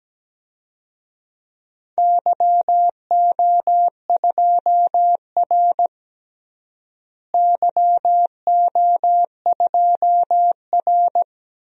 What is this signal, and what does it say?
Morse code, heard as a keyed tone, about 17 wpm, 700 Hz. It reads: YO2R YO2R
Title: DXCC-100 Call Signs - 4 Characters - Repeated 3x - 17wpm